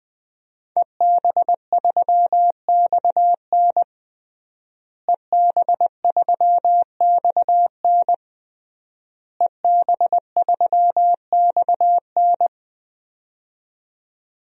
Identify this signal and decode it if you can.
Morse code, heard as a keyed tone, about 21 wpm, 700 Hz. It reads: EB3XN EB3XN EB3XN
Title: DXCC-100 Call Signs - 5 Characters - Repeated 3x - 20wpm